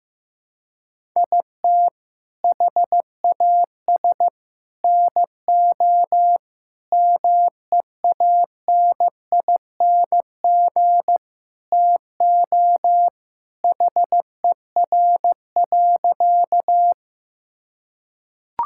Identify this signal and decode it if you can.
Morse code, heard as a keyed tone, about 16 wpm, 700 Hz. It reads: IT HAS NO MEANING TO HER.
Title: Sentences from Top 1000 Words - Review Part 6 - 15wpm